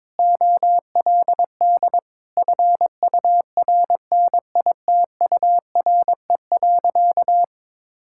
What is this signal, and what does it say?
Morse code, heard as a keyed tone, about 22 wpm, 700 Hz. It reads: OLD FURNITURE.
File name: practice / 22wpm / Sentences with 4-Word Phrases - 22wpm